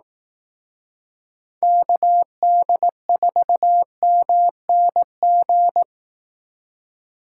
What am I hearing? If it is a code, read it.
Morse code, heard as a keyed tone, about 18 wpm, 700 Hz. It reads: KD4MNG
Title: Speed Racing - USA 2x3 Call Signs 15wpm